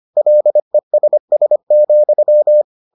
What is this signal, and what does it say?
Morse code, heard as a keyed tone, about 25 wpm, 600 Hz. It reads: LESS,